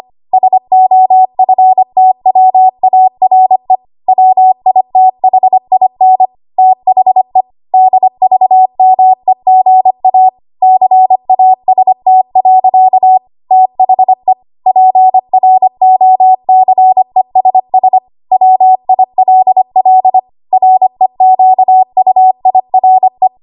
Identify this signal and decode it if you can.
Morse code, heard as a keyed tone, about 25 wpm, 750 Hz. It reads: SOFTWARE WITHIN THE DVMEGA CAST. THE PROCESS WILL REQUIRE
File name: Archive / 25 WPM / 25WPM